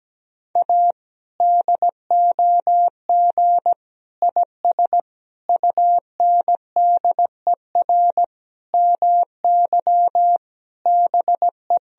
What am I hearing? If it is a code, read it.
Morse code, heard as a keyed tone, about 17 wpm, 700 Hz. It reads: A DOG IS UNDER MY BE
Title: Sentences from Top 1000 Words - Review Part 2 - 17wpm